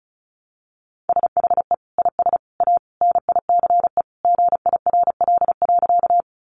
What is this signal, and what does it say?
Morse code, heard as a keyed tone, about 34 wpm, 700 Hz. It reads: SHE IS A NICE GIRL.
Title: Sentences from Top 1100 Words 35wpm